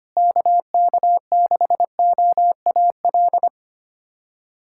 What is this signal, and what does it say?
Morse code, heard as a keyed tone, about 25 wpm, 700 Hz. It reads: KK6OAL
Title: USA Callsign DE Callsign - 6 Characters - 25wpm